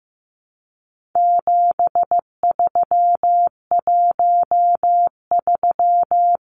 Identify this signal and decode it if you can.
Morse code, heard as a keyed tone, about 15 wpm, 700 Hz. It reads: 7313
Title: Numbers - 4-Digit 15wpm